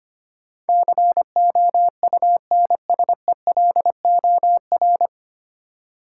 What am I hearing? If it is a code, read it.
Morse code, heard as a keyed tone, about 25 wpm, 700 Hz. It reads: COUNSELOR
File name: Nine Letter Words - 25wpm